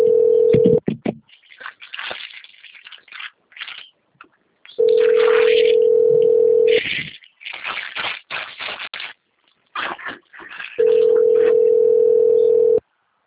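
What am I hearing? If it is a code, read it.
Ringback tone (North America), 440 + 480 Hz.
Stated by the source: Call progress tone from a telephone call recorded at the Cornelio Procopio - PR system.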